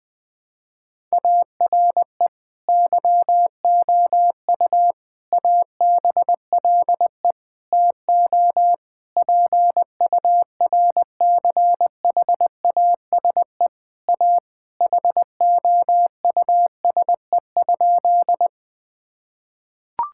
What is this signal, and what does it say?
Morse code, heard as a keyed tone, about 20 wpm, 700 Hz. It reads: ARE YOU ABLE TO PURCHASE A HOUSE?
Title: Sentences from Top 1000 Words - Review Part 5 - 20wpm